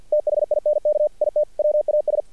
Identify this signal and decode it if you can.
Morse code, heard as a keyed tone, about 48 wpm, 600 Hz. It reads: THINK IT WAS
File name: archived / CRSnet